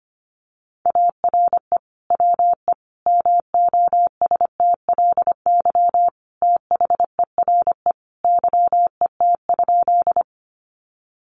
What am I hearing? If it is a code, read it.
Morse code, heard as a keyed tone, about 25 wpm, 700 Hz. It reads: ARE WE MOSTLY THERE YET?
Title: Sentences from Top 1300 Words 25wpm